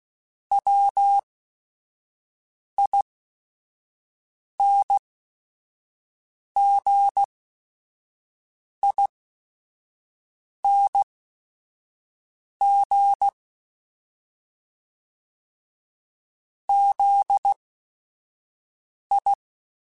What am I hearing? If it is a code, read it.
Morse code, heard as a keyed tone, about 16 wpm, spaced as for 5 wpm, 800 Hz. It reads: WINGING ZI